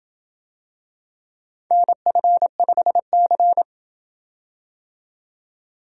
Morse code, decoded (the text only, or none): NF5C